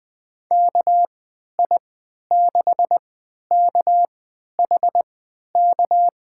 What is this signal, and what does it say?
Morse code, heard as a keyed tone, about 20 wpm, spaced as for 11 wpm, 700 Hz. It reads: KI6KHK
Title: USA Call Signs - 6 Characters - 3x Character Spacing - 20wpm